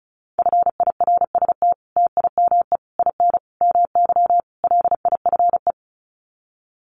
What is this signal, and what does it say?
Morse code, heard as a keyed tone, about 34 wpm, 700 Hz. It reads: FIRST TIME IN MY LIFE
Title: Sets of 5 Words Encore2 35wpm